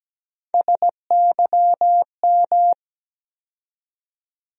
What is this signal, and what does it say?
Morse code, heard as a keyed tone, about 17 wpm, 700 Hz. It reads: SYM
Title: USA Call Sign Postfixes - 3 Letters 17wpm